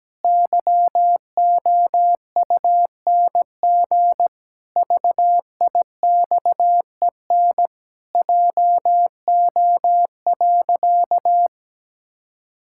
Morse code, heard as a keyed tone, about 17 wpm, 700 Hz. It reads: YOUNG VIXEN JO.
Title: Pangram Challenge v2 - 17wpm